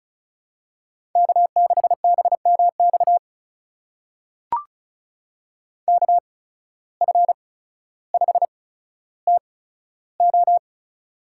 Morse code, heard as a keyed tone, about 36 wpm, spaced as for 15 wpm, 700 Hz. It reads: K6BMX KF5TO
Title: USA Call Signs - 5 Characters - 8x Character Spacing - 35wpm